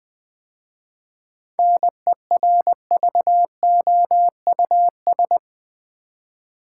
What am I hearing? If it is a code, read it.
Morse code, heard as a keyed tone, about 20 wpm, 690 Hz. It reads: NERVOUS
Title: 7 Letter Words - First 3 Letters Given - 20wpm